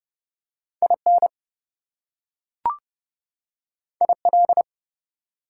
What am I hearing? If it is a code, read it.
Morse code, heard as a keyed tone, about 30 wpm, 700 Hz. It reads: IN IL
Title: US States and Canadian Provinces 30wpm